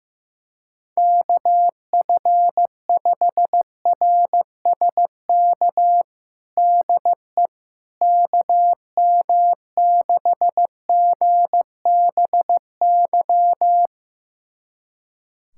Morse code, heard as a keyed tone, about 15 wpm, 700 Hz. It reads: KF5RSK DE KM6GBY
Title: USA Callsign DE Callsign - 6 Characters - 15wpm